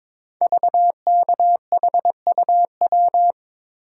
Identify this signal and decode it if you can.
Morse code, heard as a keyed tone, about 23 wpm, 700 Hz. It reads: VKHUW
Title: Code Groups - 5 Letters - 22wpm